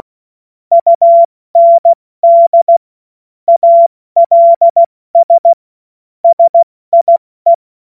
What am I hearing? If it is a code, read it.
Morse code, heard as a keyed tone, about 16 wpm, spaced as for 14 wpm, 680 Hz. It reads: UND ALS SIE
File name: Morsekurs / Witze / Witz 07. Dezember 2021, 70 F80